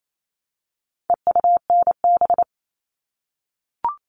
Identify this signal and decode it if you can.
Morse code, heard as a keyed tone, about 28 wpm, 700 Hz. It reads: EUNB
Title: All Cut Numbers - 4-Digit 28wpm